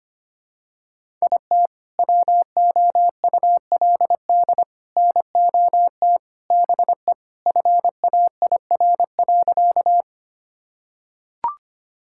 Morse code, heard as a keyed tone, about 25 wpm, 700 Hz. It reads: IT WOULD NOT BE FAIR.
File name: practice / 25wpm / Sentences with 3-Word Phrases - 25wpm